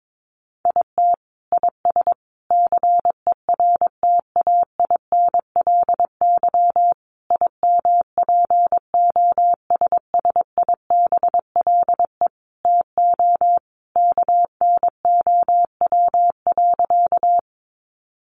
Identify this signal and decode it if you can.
Morse code, heard as a keyed tone, about 22 wpm, 700 Hz. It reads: IT IS CERTAINLY IMPOSSIBLE TO KNOW.
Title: Sentences from Top 1100 Words 22wpm